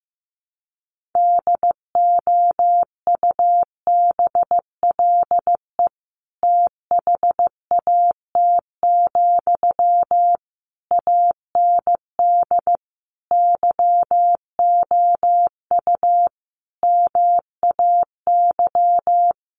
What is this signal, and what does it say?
Morse code, heard as a keyed tone, about 15 wpm, 700 Hz. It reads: DOUBLE THAT, AND YOU MAY
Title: Sentences from Top 1400 Words 15wpm